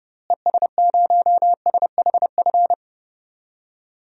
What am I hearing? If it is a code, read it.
Morse code, heard as a keyed tone, about 32 wpm, 700 Hz. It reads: ES0SHF
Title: DXCC-100 Call Signs - 6 Characters - Repeated 3x - 30wpm